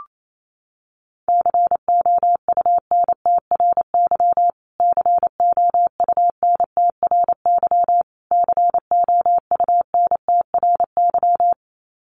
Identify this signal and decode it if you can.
Morse code, heard as a keyed tone, about 28 wpm, 700 Hz. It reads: COUNTRY COUNTRY COUNTRY
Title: Top 500 Words Repeated 28wpm